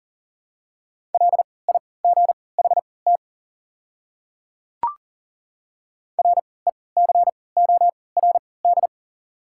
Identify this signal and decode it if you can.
Morse code, heard as a keyed tone, about 40 wpm, spaced as for 23 wpm, 700 Hz. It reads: LIGHT RECORD
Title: 3x Character Spacing - Top 500 Words - 40wpm